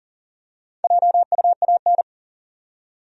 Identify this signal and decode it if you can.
Morse code, heard as a keyed tone, about 40 wpm, 700 Hz. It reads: JUAN
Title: USA Common Names - 40wpm